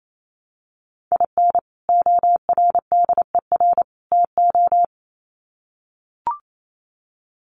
Morse code, heard as a keyed tone, about 28 wpm, 700 Hz. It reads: IN ORDER TO
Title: Sets of 3 Words 28wpm